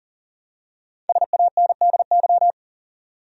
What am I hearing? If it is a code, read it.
Morse code, heard as a keyed tone, about 40 wpm, 700 Hz. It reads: SANDY
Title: USA Common Names - Max 5 Letters - 40wpm